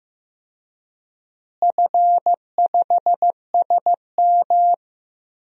Morse code, heard as a keyed tone, about 15 wpm, 700 Hz. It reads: F5SM
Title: DXCC-100 Call Signs - 4 Characters - Repeated 3x - 15wpm